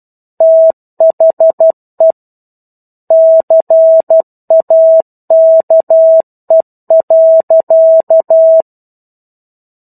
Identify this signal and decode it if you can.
Morse code, heard as a keyed tone, about 12 wpm, 650 Hz. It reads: THE CAKE.
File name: 01-Chapter-01 12 WPM